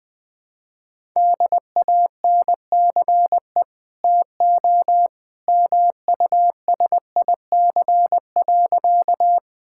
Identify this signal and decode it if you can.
Morse code, heard as a keyed tone, about 20 wpm, 700 Hz. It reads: DANCE TO MUSIC.
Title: Easy Sentences with Hints - 20wpm